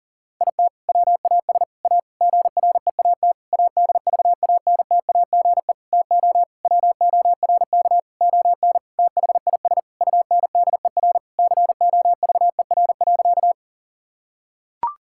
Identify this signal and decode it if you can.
Morse code, heard as a keyed tone, about 37 wpm, 700 Hz. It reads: IT WAS A GREAT ADVANTAGE TO WORK ON THIS UNDER COVER.
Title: Sentences from Top 1000 Words - Review Part 6 - 40wpm